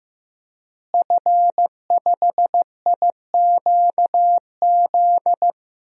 Morse code, heard as a keyed tone, about 15 wpm, 700 Hz. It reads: F5IQZ
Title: DXCC-100 Call Signs - 5 Characters - Repeated 2x - 15wpm